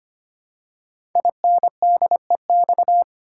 Morse code, heard as a keyed tone, about 25 wpm, 700 Hz. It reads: INDEX
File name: 5 Letter Words - First 3 Letters Given - 25wpm